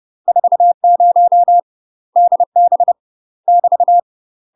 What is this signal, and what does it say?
Morse code, heard as a keyed tone, about 30 wpm, 700 Hz. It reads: 40 DB <BT>
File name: UK MASTER QSOs TRAINING / 30WPM / UK MASTER QSOs 0020